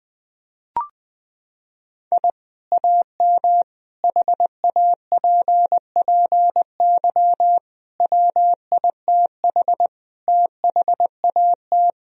Morse code, heard as a keyed tone, about 20 wpm, 700 Hz. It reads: I AM HAPPY WITH THAT
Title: Sentences from Top 1000 Words - Review Part 2 - 20wpm